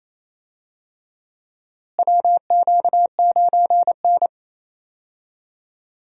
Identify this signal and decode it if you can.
Morse code, heard as a keyed tone, about 28 wpm, 700 Hz. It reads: WQ9N
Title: USA Call Signs - 4 Characters - Repeated 2x - 28wpm